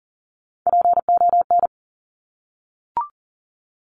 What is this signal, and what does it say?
Morse code, heard as a keyed tone, about 41 wpm, 700 Hz. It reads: PON
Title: Top Three Letter Combinations in Words 40wpm